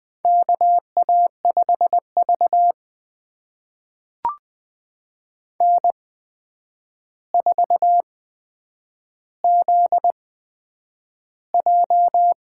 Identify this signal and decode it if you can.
Morse code, heard as a keyed tone, about 20 wpm, spaced as for 9 wpm, 700 Hz. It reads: KA5V N4ZJ
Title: USA Call Signs - 4 Characters - 8x Character Spacing - 20wpm